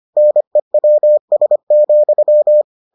Morse code, heard as a keyed tone, about 25 wpm, 600 Hz. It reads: NEWS,